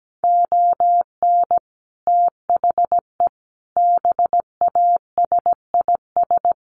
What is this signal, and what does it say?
Morse code, heard as a keyed tone, about 17 wpm, 700 Hz. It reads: ON THE BASIS
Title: Sets of 5 Words 17wpm